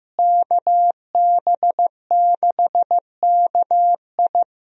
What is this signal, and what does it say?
Morse code, heard as a keyed tone, about 16 wpm, 700 Hz. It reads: KB6KI
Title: USA Call Signs - 5 Characters - Repeated 3x - 15wpm